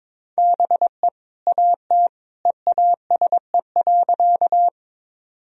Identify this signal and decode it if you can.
Morse code, heard as a keyed tone, about 22 wpm, 700 Hz. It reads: BE AT EASE.
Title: Sentences from Top 1000 Words - Review Part 1 - 22wpm